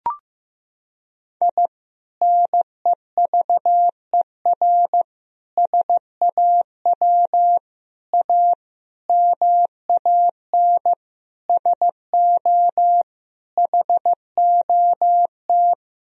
Morse code, heard as a keyed tone, about 15 wpm, 700 Hz. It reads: I NEVER SAW A MAN SO HOT
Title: Sentences from Top 1000 Words - Review Part 3 - 15wpm